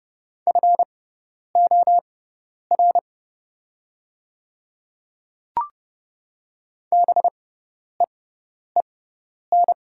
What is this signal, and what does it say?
Morse code, heard as a keyed tone, about 30 wpm, spaced as for 10 wpm, 700 Hz. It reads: FOR BEEN
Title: 6x Character Spacing - Top 100 Words - 30wpm